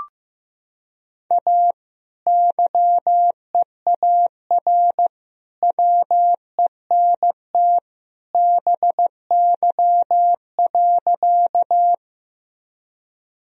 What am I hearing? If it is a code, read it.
Morse code, heard as a keyed tone, about 15 wpm, 700 Hz. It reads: A YEAR WENT BY.
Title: Sentences from Top 1000 Words - Review Part 1 - 15wpm